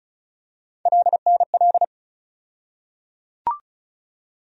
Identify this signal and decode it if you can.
Morse code, heard as a keyed tone, about 36 wpm, 700 Hz. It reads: LNL
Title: USA Call Sign Postfixes - 3 Letters 35wpm